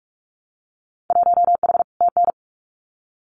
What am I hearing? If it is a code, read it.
Morse code, heard as a keyed tone, about 46 wpm, 700 Hz. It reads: 1H TN
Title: Winter Field Day Exchanges 45wpm